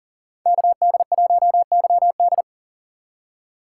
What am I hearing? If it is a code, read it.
Morse code, heard as a keyed tone, about 40 wpm, 700 Hz. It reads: KD1YD
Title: USA Call Signs - 5 Characters - 2x Character Spacing - 40wpm